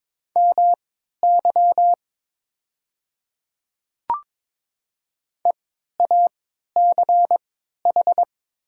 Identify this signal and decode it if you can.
Morse code, heard as a keyed tone, about 22 wpm, spaced as for 12 wpm, 690 Hz. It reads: MY EACH
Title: 3x Character Spacing - Top 100 Words - 22wpm